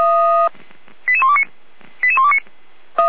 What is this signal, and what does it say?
Five-tone selective call (ZVEI), heard as quick runs of five tones, tones of about 70 ms. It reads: 88129 88129